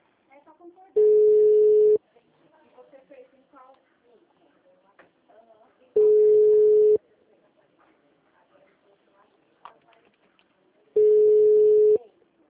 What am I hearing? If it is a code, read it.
Ringback tone (Europe, 425 Hz), with speech.